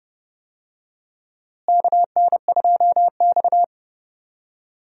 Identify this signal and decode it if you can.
Morse code, heard as a keyed tone, about 30 wpm, 700 Hz. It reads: KN2X
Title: USA Call Signs - 4 Characters - Repeated 3x - 30wpm